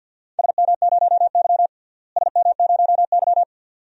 Morse code, heard as a keyed tone, about 52 wpm, 700 Hz. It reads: SM0Y SM0Y
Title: DXCC-100 Call Signs - 4 Characters - Repeated 2x - 50wpm